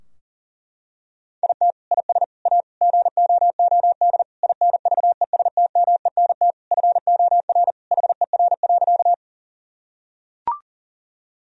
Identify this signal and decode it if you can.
Morse code, heard as a keyed tone, about 40 wpm, 700 Hz. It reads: IT IS A GOOD INVESTMENT FOR HER.
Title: Sentences from Top 1000 Words - Review Part 4 - 40wpm